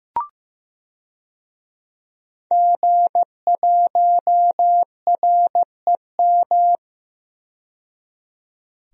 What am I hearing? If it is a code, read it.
Morse code, heard as a keyed tone, about 15 wpm, 700 Hz. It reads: G1REM
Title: DXCC-100 Call Signs - 5 Characters - 15wpm